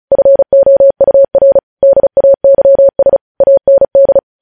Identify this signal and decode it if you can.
Morse code, heard as a keyed tone, about 36 wpm, 550 Hz. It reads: FOUR DAYS AND
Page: news headlines in morse code at 35 wpm